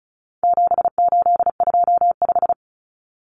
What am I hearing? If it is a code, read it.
Morse code, heard as a keyed tone, about 34 wpm, 700 Hz. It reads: 7825
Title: Numbers - 4-Digit 35wpm